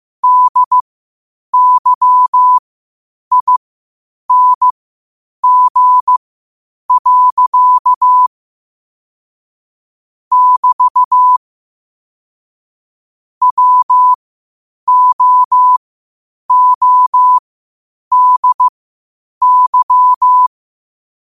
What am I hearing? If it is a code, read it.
Morse code, heard as a keyed tone, about 15 wpm, spaced as for 8 wpm, 1000 Hz. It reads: DYING. <BT> WOODY
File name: qotdfiles / 8wpm